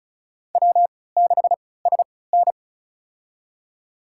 Morse code, heard as a keyed tone, about 34 wpm, spaced as for 20 wpm, 700 Hz. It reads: W6SN
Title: USA Call Signs - 4 Characters - 3x Character Spacing - 35wpm